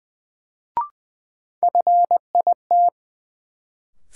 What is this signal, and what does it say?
Morse code, heard as a keyed tone, about 20 wpm, 700 Hz. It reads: FIT
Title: Sentences from Top 800 Words - New Word Drill - 20wpm